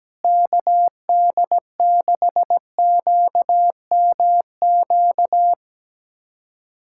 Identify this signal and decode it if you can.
Morse code, heard as a keyed tone, about 17 wpm, 700 Hz. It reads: KD6QMQ
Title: USA Callsign DE Callsign - 6 Characters - 17wpm